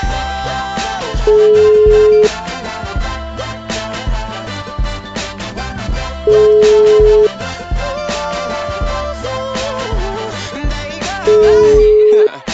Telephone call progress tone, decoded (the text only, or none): ringback tone